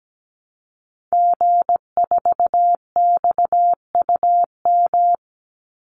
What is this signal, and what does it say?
Morse code, heard as a keyed tone, about 17 wpm, 700 Hz. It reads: G4XUM
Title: Call Signs - 17wpm